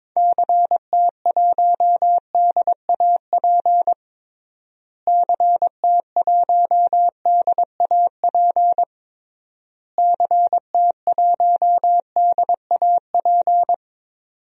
Morse code, heard as a keyed tone, about 21 wpm, 700 Hz. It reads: CT1DAP CT1DAP CT1DAP
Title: DXCC-100 Call Signs - 6 Characters - Repeated 3x - 22wpm